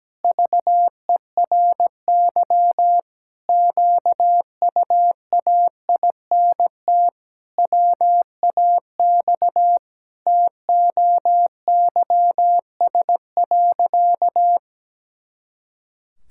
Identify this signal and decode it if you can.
Morse code, heard as a keyed tone, about 17 wpm, 700 Hz. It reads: VERY QUAINT WAX TOYS.
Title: Pangram Challenge v2 - 17wpm